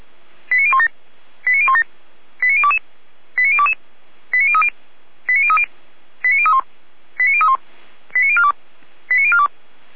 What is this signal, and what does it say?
Five-tone selective call (ZVEI), heard as quick runs of five tones, tones of about 70 ms. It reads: 89017 89017 89022 89022 89030 89030 89031 89031 89042 89042